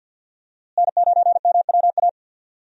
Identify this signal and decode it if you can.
Morse code, heard as a keyed tone, about 50 wpm, 700 Hz. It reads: N9MWA
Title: USA Call Signs - 5 Characters - Repeated 3x - 50wpm